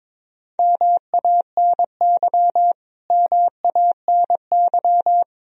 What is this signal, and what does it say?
Morse code, heard as a keyed tone, about 22 wpm, 700 Hz. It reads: MANY MANY
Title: Top 100 Words Repeated 22wpm